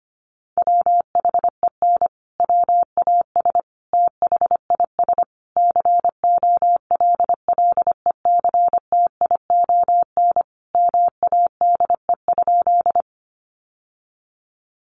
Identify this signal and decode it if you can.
Morse code, heard as a keyed tone, about 25 wpm, 700 Hz. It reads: WHEN WAS THIS COLLECTION MADE?
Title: Sentences from Top 1200 Words 25wpm